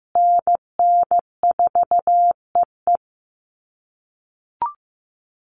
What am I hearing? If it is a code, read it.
Morse code, heard as a keyed tone, about 15 wpm, 700 Hz. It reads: NN4EE
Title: USA Call Signs Repeated 2x 15wpm